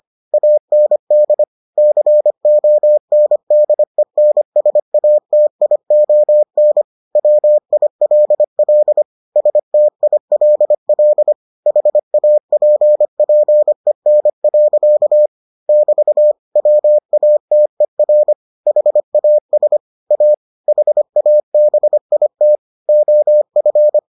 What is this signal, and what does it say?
Morse code, heard as a keyed tone, about 25 wpm, 600 Hz. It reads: AND CONDENSATION WILL STILL HAPPEN. <BT> WATER HAS A HABIT OF